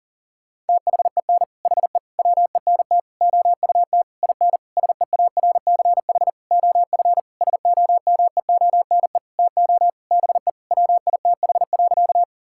Morse code, heard as a keyed tone, about 40 wpm, 700 Hz. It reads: THEN HE WENT OUT IN SEARCH OF SOMEONE TO BE WITH.
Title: Sentences from Top 1000 Words - Review Part 5 - 40wpm